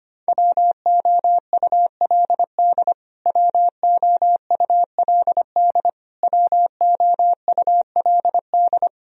Morse code, heard as a keyed tone, about 25 wpm, 700 Hz. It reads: WOULD WOULD WOULD
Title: Top 100 Words Repeated 25wpm